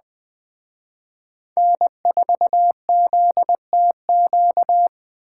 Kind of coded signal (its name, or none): Morse code